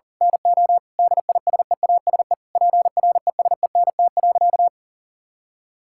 Morse code, heard as a keyed tone, about 39 wpm, 700 Hz. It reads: NO DISEASE PRESENT.
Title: Sentences from Top 1000 Words - Review Part 6 - 40wpm